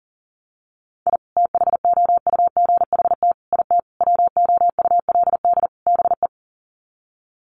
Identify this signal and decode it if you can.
Morse code, heard as a keyed tone, about 37 wpm, 700 Hz. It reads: I THOUGHT IT WOULD BE
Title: Sets of 5 Words Encore 40wpm